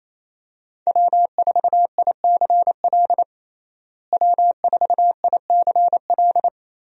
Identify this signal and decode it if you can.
Morse code, heard as a keyed tone, about 28 wpm, 700 Hz. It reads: W4ICL W4ICL
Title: USA Call Signs - 5 Characters - Repeated 3x - 28wpm